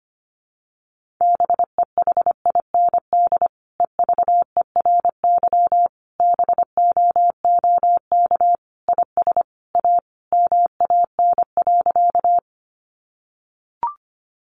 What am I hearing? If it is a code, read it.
Morse code, heard as a keyed tone, about 25 wpm, 700 Hz. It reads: BEHIND EVERY BOOK IS A MAN.
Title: Sentences from Top 500 Words 25wpm